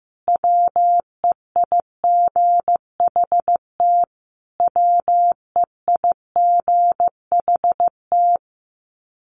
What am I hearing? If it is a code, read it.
Morse code, heard as a keyed tone, about 15 wpm, 700 Hz. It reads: WEIGHT WEIGHT
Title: Top 500 Words Repeated 15wpm